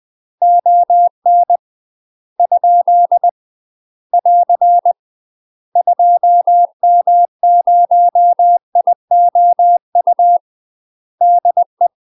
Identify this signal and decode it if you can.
Morse code, heard as a keyed tone, about 20 wpm, 700 Hz. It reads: ON ? <AR> 2M0IOU DE
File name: UK MASTER QSOs TRAINING/20WPM